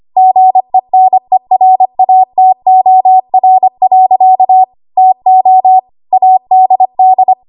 Morse code, heard as a keyed tone, about 25 wpm, 750 Hz. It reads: GENERATOR. TO ADD
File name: Archive / 25 WPM / 25WPM